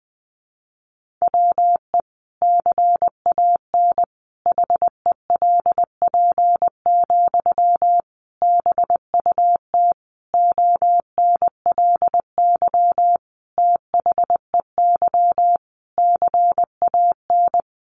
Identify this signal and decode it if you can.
Morse code, heard as a keyed tone, about 20 wpm, 700 Hz. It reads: WE CAN HELP, BUT ONLY THEY CAN
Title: Sentences from Top 1500 Words 20wpm